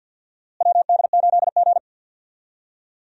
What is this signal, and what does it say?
Morse code, heard as a keyed tone, about 53 wpm, 700 Hz. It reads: WD8G
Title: USA Call Signs - 4 Characters - Repeated 2x - 50wpm